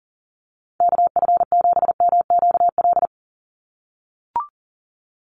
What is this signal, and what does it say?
Morse code, heard as a keyed tone, about 40 wpm, 700 Hz. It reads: KF7MQL
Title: USA Call Signs 40wpm